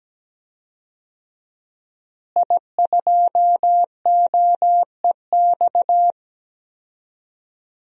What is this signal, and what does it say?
Morse code, heard as a keyed tone, about 17 wpm, 700 Hz. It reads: I2OEX
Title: DXCC-100 Call Signs - 5 Characters - Repeated 2x - 17wpm